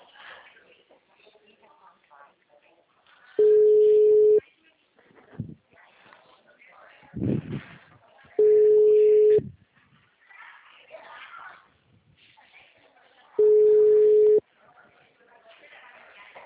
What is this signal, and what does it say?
Ringback tone (Europe, 425 Hz), with speech.